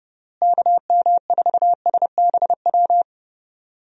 Morse code, heard as a keyed tone, about 32 wpm, 700 Hz. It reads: KM4SBW
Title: USA Call Signs - 6 Characters - Repeated 3x - 30wpm